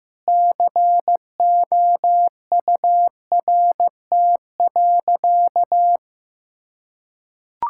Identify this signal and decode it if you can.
Morse code, heard as a keyed tone, about 15 wpm, 700 Hz. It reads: COURT.
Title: Sentences from Top 1600 Words 15wpm